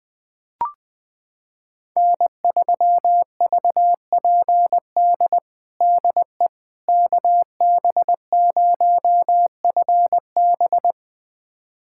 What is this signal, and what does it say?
Morse code, heard as a keyed tone, about 21 wpm, 700 Hz. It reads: N3VPD DE KB0FB
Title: USA Callsign DE Callsign - 5 Characters - 20wpm